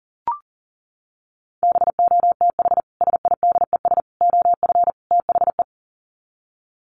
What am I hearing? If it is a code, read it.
Morse code, heard as a keyed tone, about 40 wpm, 700 Hz. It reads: BOTH SIDES OF THE
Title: Sets of 4 Words 40wpm